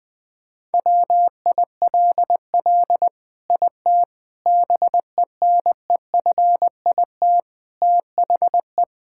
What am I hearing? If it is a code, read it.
Morse code, heard as a keyed tone, about 21 wpm, 700 Hz. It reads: WILL IT BENEFIT THE
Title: Sentences from Top 1000 Words - Review Part 6 - 20wpm